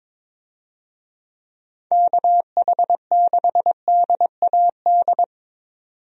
Morse code, heard as a keyed tone, about 22 wpm, 700 Hz. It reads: KH6DAD
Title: DXCC-100 Call Signs - 6 Characters - 22wpm